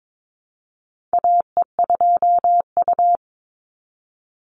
Morse code, heard as a keyed tone, about 22 wpm, 700 Hz. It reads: AE2U
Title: Call Signs - 22wpm